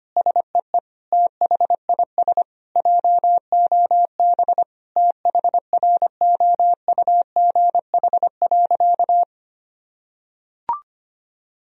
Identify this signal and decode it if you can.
Morse code, heard as a keyed tone, about 25 wpm, 700 Hz. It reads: SEE THIS JOB THROUGH.
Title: Sentences from Top 1000 Words - Review Part 3 - 25wpm